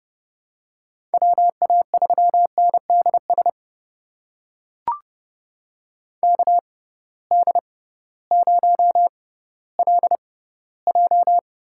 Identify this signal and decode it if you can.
Morse code, heard as a keyed tone, about 30 wpm, spaced as for 16 wpm, 700 Hz. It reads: WA3NDS KD0LJ
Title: USA Call Signs - 6 Characters - 6x Character Spacing - 30wpm